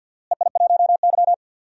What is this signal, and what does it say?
Morse code, heard as a keyed tone, about 45 wpm, 700 Hz. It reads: EI1Y
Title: DXCC-100 Call Signs - 4 Characters - Repeated 2x - 50wpm